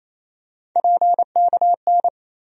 Morse code, heard as a keyed tone, about 28 wpm, 700 Hz. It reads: PKN
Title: USA Call Sign Postfixes - 3 Letters 28wpm